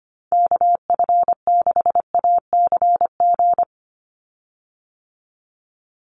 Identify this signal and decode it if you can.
Morse code, heard as a keyed tone, about 25 wpm, 700 Hz. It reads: KF6ACG
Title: USA Call Signs 25wpm